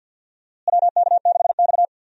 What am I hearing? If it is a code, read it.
Morse code, heard as a keyed tone, about 48 wpm, 700 Hz. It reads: WK6X
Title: USA Call Signs - 4 Characters - Repeated 2x - 50wpm